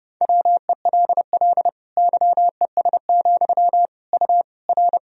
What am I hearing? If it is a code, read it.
Morse code, heard as a keyed tone, about 29 wpm, 700 Hz. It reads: WELL YES, U R
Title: Sentences from Top 500 Words and Abbreviations 30wpm